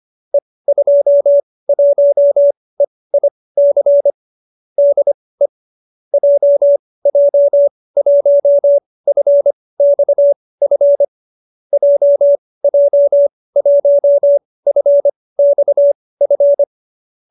Morse code, heard as a keyed tone, about 25 wpm, spaced as for 18 wpm, 570 Hz. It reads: E21EIC DE JJ1FXF JJ1FXF